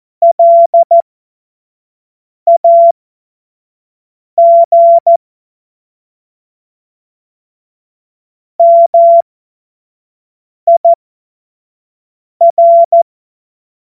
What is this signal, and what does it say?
Morse code, heard as a keyed tone, about 14 wpm, spaced as for 5 wpm, 680 Hz. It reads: LAG MIR
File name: Morsekurs / Witze / Witz 04. Oktober 2022 25 F70